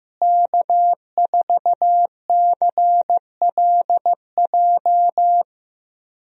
Morse code, heard as a keyed tone, about 15 wpm, 700 Hz. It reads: K4CLJ
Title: USA Call Signs - 5 Characters - Repeated 3x - 15wpm